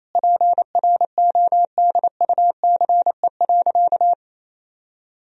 Morse code, heard as a keyed tone, about 28 wpm, 700 Hz. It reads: PRODUCE.
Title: Sentences from Top 1000 Words - Review Part 3 - 28wpm